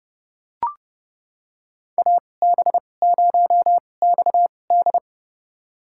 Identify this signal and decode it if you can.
Morse code, heard as a keyed tone, about 30 wpm, spaced as for 22 wpm, 700 Hz. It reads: AB0XD
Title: USA Call Signs - 5 Characters - 2x Character Spacing - 30wpm